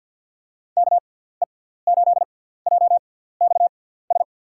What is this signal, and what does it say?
Morse code, heard as a keyed tone, about 53 wpm, spaced as for 17 wpm, 700 Hz. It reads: KE8JXS
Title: USA Call Signs - 6x Character Spacing - 50wpm